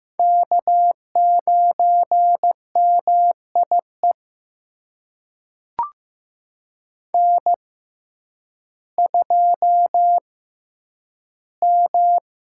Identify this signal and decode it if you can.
Morse code, heard as a keyed tone, about 15 wpm, spaced as for 9 wpm, 700 Hz. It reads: K9MIE N2M
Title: USA Call Signs - 5 Characters - 6x Character Spacing - 15wpm